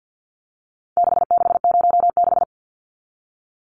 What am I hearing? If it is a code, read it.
Morse code, heard as a keyed tone, about 50 wpm, 700 Hz. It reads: 6606
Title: Numbers - 4-Digit 50wpm